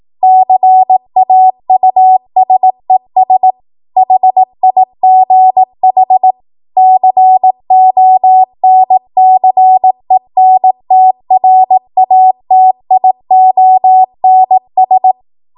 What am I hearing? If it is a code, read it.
Morse code, heard as a keyed tone, about 18 wpm, 750 Hz. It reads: CAUSES HIGH CONCENTRATIONS
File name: Morse / 18WPM